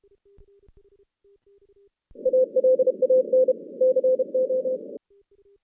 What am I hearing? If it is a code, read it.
Morse code, heard as a keyed tone, about 31 wpm, 520 Hz. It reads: ALAN CO
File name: 2016 JAN CW NAQP / 2016 Jan naqp k5ot - 0-10